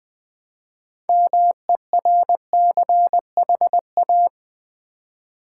Morse code, heard as a keyed tone, about 20 wpm, 700 Hz. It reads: MERCHA
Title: Build A Word - 3 Letters At A Time - 8-Letter Words - 20wpm